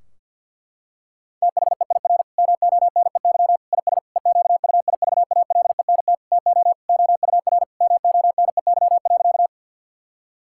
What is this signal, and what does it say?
Morse code, heard as a keyed tone, about 50 wpm, 700 Hz. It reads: THEIR MONEY IS EQUIVALENT TO OUR MONEY.
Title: Sentences from Top 1600 Words 50wpm